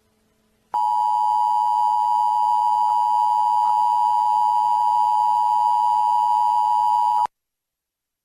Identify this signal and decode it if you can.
Emergency Alert System (SAME) tones (the 853 and 960 Hz attention tone). It is heard from 0.7 to 7.3 s.